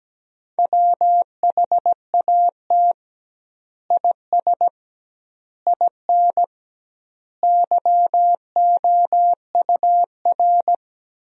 Morse code, heard as a keyed tone, about 17 wpm, 700 Hz. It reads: WHAT IS IN YOUR
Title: 2x Word-Spacing - Sentences from Top 200 Words - 17wpm